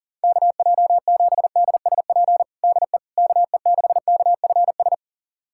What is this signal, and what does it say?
Morse code, heard as a keyed tone, about 40 wpm, 700 Hz. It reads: KJ7DSP DE KE6KFS
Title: USA Callsign DE Callsign - 6 Characters - 40wpm